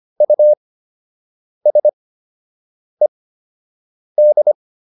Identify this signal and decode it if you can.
Morse code, heard as a keyed tone, about 25 wpm, spaced as for 7 wpm, 600 Hz. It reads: USED